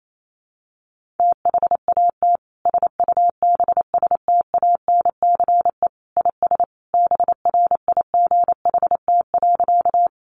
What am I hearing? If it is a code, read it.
Morse code, heard as a keyed tone, about 28 wpm, 700 Hz. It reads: THAT SUBSTANCE IS BRIGHT.
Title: Sentences from Top 1500 Words 28wpm